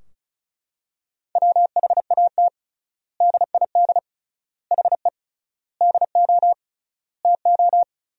Morse code, heard as a keyed tone, about 36 wpm, 700 Hz. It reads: WHAT DID HE DO TO
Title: 3x Word-Spacing - Sentences from Top 100 Words - 35wpm